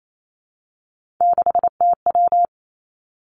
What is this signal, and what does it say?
Morse code, heard as a keyed tone, about 28 wpm, 700 Hz. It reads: 6TW
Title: Three Challenging Letter-Number Combinations 28wpm